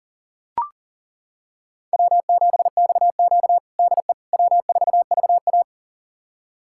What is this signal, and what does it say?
Morse code, heard as a keyed tone, about 37 wpm, 700 Hz. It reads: W7XQ DE W4VA
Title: USA Callsign DE Callsign - 4 Characters - 40wpm